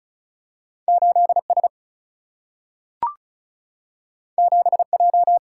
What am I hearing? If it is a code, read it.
Morse code, heard as a keyed tone, about 36 wpm, 700 Hz. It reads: 8S 7J
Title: Single-Letter Single-Number 35wpm